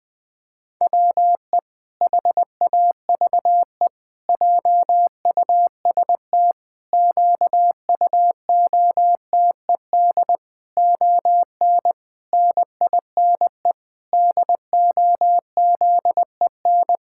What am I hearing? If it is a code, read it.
Morse code, heard as a keyed tone, about 20 wpm, 700 Hz. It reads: WE HAVE JUST QUOTED ON NINE DOZEN
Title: Pangram Challenge v2 - 20wpm